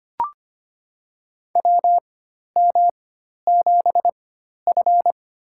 Morse code, heard as a keyed tone, about 25 wpm, spaced as for 12 wpm, 700 Hz. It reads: WM7F
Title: USA Call Signs - 4 Characters - 4x Character Spacing - 25wpm